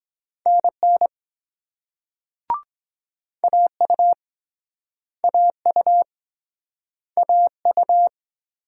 Morse code, heard as a keyed tone, about 24 wpm, 700 Hz. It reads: NN AU AU AU